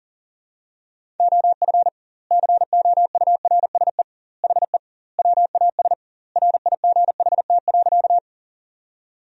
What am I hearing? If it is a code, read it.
Morse code, heard as a keyed tone, about 40 wpm, 700 Hz. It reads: OF COURSE HE WAS RIGHT.
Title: 2x Word-Spacing - Sentences from Top 500 Words - 40wpm